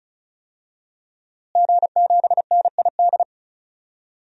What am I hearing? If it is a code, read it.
Morse code, heard as a keyed tone, about 34 wpm, 700 Hz. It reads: G7NID
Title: DXCC-100 Call Signs - 5 Characters - Repeated 3x - 35wpm